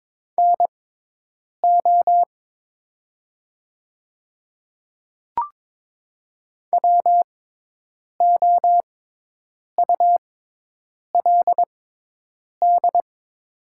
Morse code, heard as a keyed tone, about 22 wpm, spaced as for 8 wpm, 700 Hz. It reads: NO WOULD
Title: 6x Character Spacing - Top 300 Words - 22wpm